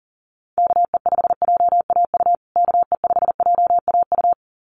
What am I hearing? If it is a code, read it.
Morse code, heard as a keyed tone, about 40 wpm, 700 Hz. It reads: KE5JAU KE5JAU
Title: USA Call Signs Repeated 2x 40wpm